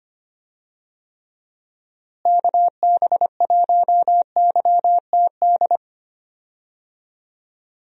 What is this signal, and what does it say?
Morse code, heard as a keyed tone, about 25 wpm, 700 Hz. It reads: KB1YTD
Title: USA Call Signs - 6 Characters - Repeated 2x - 25wpm